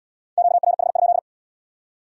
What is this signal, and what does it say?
Morse code, heard as a keyed tone, about 75 wpm, 700 Hz. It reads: QRH?